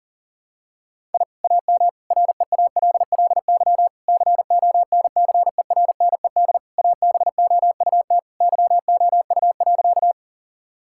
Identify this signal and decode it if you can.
Morse code, heard as a keyed tone, about 40 wpm, 700 Hz. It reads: I AM REALLY CONCERNED ABOUT YOU.
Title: Sentences from Top 1000 Words - Review Part 6 - 40wpm